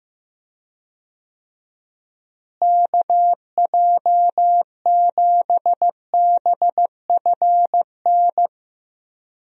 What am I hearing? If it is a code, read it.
Morse code, heard as a keyed tone, about 15 wpm, 700 Hz. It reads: KJ7BFN
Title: USA Call Signs - 6 Characters - Repeated 3x - 15wpm